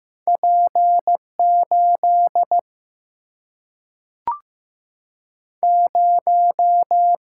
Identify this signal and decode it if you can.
Morse code, heard as a keyed tone, about 15 wpm, 700 Hz. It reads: P8 0
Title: Single-Letter Single-Number 15wpm